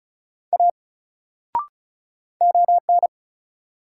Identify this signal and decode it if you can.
Morse code, heard as a keyed tone, about 36 wpm, 700 Hz. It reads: A ON
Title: Sentences from Top 100 Words - New Word Drill - 35wpm